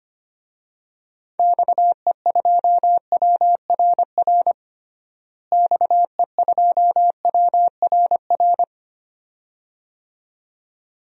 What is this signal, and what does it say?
Morse code, heard as a keyed tone, about 25 wpm, 700 Hz. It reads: XE2WRR XE2WRR
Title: DXCC-100 Call Signs - 6 Characters - Repeated 2x - 25wpm